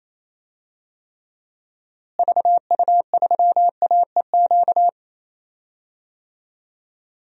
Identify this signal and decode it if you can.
Morse code, heard as a keyed tone, about 28 wpm, 700 Hz. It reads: VU3AEQ
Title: DXCC-100 Call Signs - 6 Characters - Repeated 3x - 28wpm